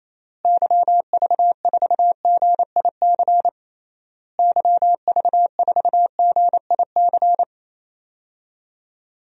Morse code, heard as a keyed tone, about 28 wpm, 700 Hz. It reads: YV4GIC YV4GIC
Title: DXCC-100 Call Signs - 6 Characters - Repeated 3x - 28wpm